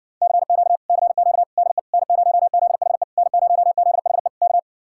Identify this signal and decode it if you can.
Morse code, heard as a keyed tone, about 60 wpm, 700 Hz. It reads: CQ CQ DE N0ZHE N0ZHE K